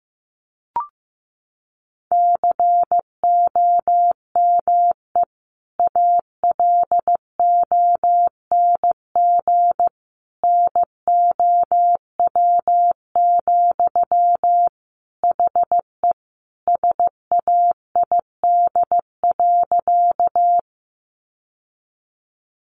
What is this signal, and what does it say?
Morse code, heard as a keyed tone, about 15 wpm, 700 Hz. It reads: COME ALONG NOW, HE SAID.
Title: Sentences from Top 700 Words 15wpm